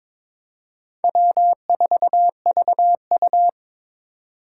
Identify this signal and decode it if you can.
Morse code, heard as a keyed tone, about 25 wpm, 700 Hz. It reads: W4VU
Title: Speed Racing - USA 1x2 Call Signs 22wpm